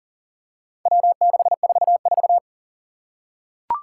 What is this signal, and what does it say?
Morse code, heard as a keyed tone, about 39 wpm, 700 Hz. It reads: W644
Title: Four Challenging Letter-Number Combinations 40wpm